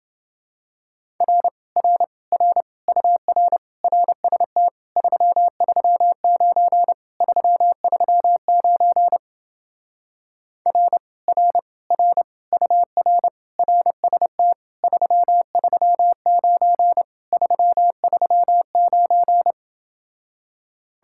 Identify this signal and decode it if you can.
Morse code, heard as a keyed tone, about 28 wpm, 700 Hz. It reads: R R R UR RST 339 339 R R R UR RST 339 339